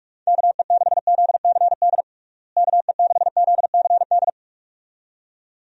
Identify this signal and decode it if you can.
Morse code, heard as a keyed tone, about 41 wpm, 700 Hz. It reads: KE6ZCD KE6ZCD
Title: USA Call Signs - 6 Characters - Repeated 2x - 45wpm